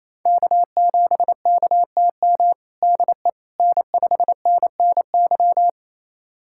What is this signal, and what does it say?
Morse code, heard as a keyed tone, about 28 wpm, 700 Hz. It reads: K7KTM DE N5NNY
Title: USA Callsign DE Callsign 28wpm